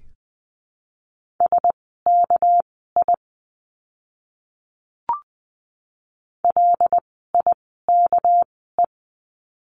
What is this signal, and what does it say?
Morse code, heard as a keyed tone, about 20 wpm, spaced as for 15 wpm, 700 Hz. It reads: SKI LIKE
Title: 199 Lesson 32f - Letter k - Words 20wpm